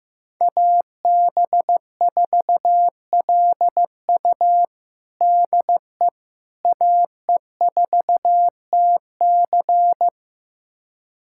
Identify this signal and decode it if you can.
Morse code, heard as a keyed tone, about 15 wpm, 700 Hz. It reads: AB4LU DE AE4TC
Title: USA Callsign DE Callsign - 5 Characters - 15wpm